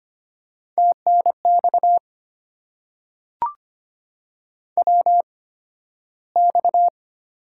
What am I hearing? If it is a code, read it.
Morse code, heard as a keyed tone, about 25 wpm, spaced as for 13 wpm, 700 Hz. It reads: TNX WX
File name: Top 90 QSO Elements - 8x Character Spacing - 25wpm